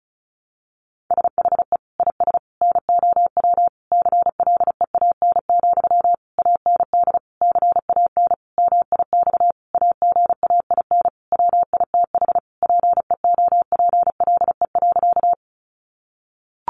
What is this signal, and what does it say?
Morse code, heard as a keyed tone, about 34 wpm, 700 Hz. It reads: SHE IS NOW CLEAN, AND CAN MIX AGAIN WITH PEOPLE.
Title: Sentences from Top 1100 Words 35wpm